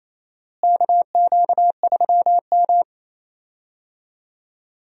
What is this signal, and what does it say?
Morse code, heard as a keyed tone, about 28 wpm, 700 Hz. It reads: KQ3M
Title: USA Call Signs - 4 Characters - Repeated 3x - 28wpm